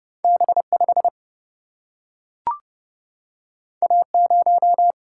Morse code, heard as a keyed tone, about 29 wpm, 700 Hz. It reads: B5 A0
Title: Single-Letter Single-Number 30wpm